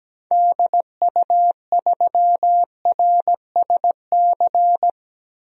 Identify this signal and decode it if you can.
Morse code, heard as a keyed tone, about 17 wpm, 700 Hz. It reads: DU3RSC
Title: DXCC-100 Call Signs - 6 Characters - 17wpm